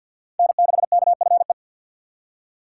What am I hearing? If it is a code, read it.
Morse code, heard as a keyed tone, about 52 wpm, 700 Hz. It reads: N6KFE
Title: USA Call Signs - 5 Characters - 2x Character Spacing - 50wpm